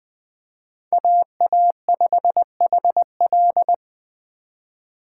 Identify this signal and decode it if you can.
Morse code, heard as a keyed tone, about 21 wpm, 700 Hz. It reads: AA5HL
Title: Speed Racing - USA 2x2 Call Signs 17wpm